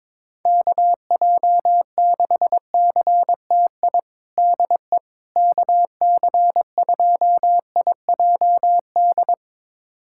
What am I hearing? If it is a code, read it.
Morse code, heard as a keyed tone, about 21 wpm, 700 Hz. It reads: KJ6CTI DE KC2IJD
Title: USA Callsign DE Callsign - 6 Characters - 22wpm